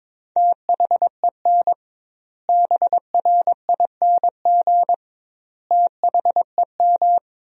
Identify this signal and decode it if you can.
Morse code, heard as a keyed tone, about 22 wpm, 700 Hz. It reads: THEN BRING THEM
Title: 2x Word-Spacing - Sentences from Top 500 Words - 22wpm